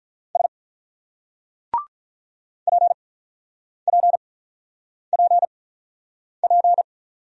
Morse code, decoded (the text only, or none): S PPPP